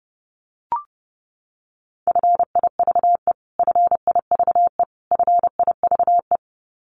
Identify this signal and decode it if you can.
Morse code, heard as a keyed tone, about 30 wpm, 700 Hz. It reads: FIVE FIVE FIVE
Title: Top 500 Words Repeated 30wpm